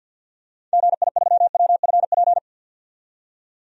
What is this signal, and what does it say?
Morse code, heard as a keyed tone, about 48 wpm, 700 Hz. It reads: GI3WFP
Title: DXCC-100 Call Signs - 6 Characters - 50wpm